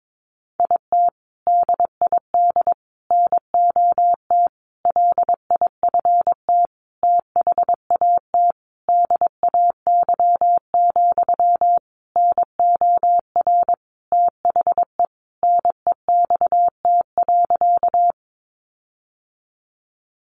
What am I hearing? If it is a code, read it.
Morse code, heard as a keyed tone, about 22 wpm, 700 Hz. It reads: IT DID NOT LIFT THAT DAY, NOR THE NEXT.
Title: Sentences from Top 1500 Words 22wpm